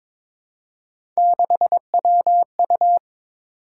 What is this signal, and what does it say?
Morse code, heard as a keyed tone, about 22 wpm, 700 Hz. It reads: 6WU
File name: practice / 22wpm / Three Letter-Number Combinations - 22wpm